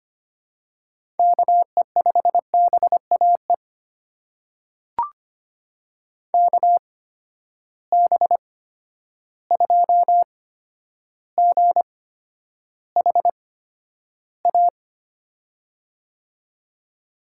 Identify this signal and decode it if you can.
Morse code, heard as a keyed tone, about 25 wpm, spaced as for 11 wpm, 700 Hz. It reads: KE5BAE KB2GHA
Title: USA Call Signs - 6 Characters - 8x Character Spacing - 25wpm